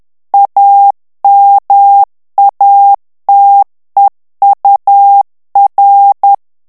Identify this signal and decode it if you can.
Morse code, heard as a keyed tone, about 10 wpm, 800 Hz. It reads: AMATEUR
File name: ANGLAIS texte clair 3 vitesse 10 mots minute